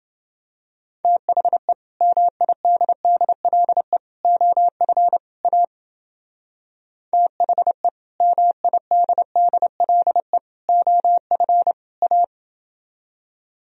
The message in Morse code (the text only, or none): THE MIDDLE OF A THE MIDDLE OF A